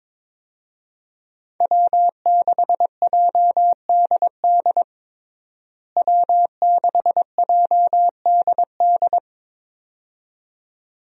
Morse code, heard as a keyed tone, about 22 wpm, 700 Hz. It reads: W6JDD W6JDD
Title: USA Call Signs - 5 Characters - Repeated 2x - 22wpm